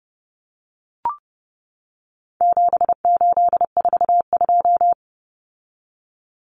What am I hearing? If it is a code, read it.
Morse code, heard as a keyed tone, about 30 wpm, 700 Hz. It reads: 7842
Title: Numbers - 4-Digit 30wpm